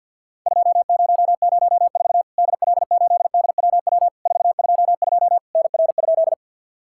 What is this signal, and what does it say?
Morse code, heard as a keyed tone, about 50 wpm, 700 Hz. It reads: 2004 DL8DWW 422 NR?
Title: Worked All Europe DX Contest - QTC Exchange with Repeats - 50wpm